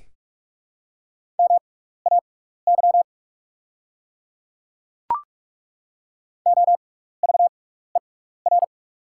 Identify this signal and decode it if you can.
Morse code, heard as a keyed tone, about 43 wpm, spaced as for 15 wpm, 700 Hz. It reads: MAY OVER
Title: 6x Character Spacing - Top 100 Words - 45wpm